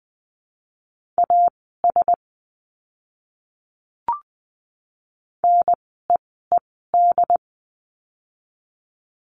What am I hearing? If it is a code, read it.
Morse code, heard as a keyed tone, about 20 wpm, spaced as for 14 wpm, 700 Hz. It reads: AS NEED
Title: 115 Lesson 19f - Letter w - Words 20wpm